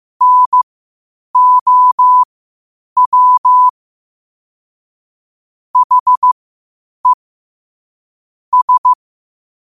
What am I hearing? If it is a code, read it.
Morse code, heard as a keyed tone, about 15 wpm, spaced as for 8 wpm, 1000 Hz. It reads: NOW HE S
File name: qotdfiles / 8wpm